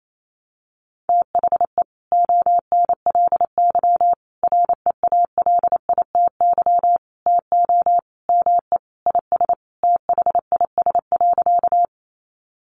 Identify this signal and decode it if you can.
Morse code, heard as a keyed tone, about 28 wpm, 700 Hz. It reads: THE ONLY REALITY TO ME IS THIS.
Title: Sentences from Top 1200 Words 28wpm